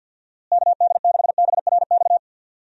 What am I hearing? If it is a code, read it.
Morse code, heard as a keyed tone, about 48 wpm, 700 Hz. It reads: KD6BRX
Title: USA Call Signs - 6 Characters - Repeated 3x - 50wpm